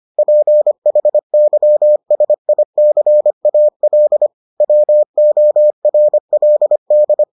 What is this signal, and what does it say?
Morse code, heard as a keyed tone, about 25 wpm, 600 Hz. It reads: PHYSICAL WORLD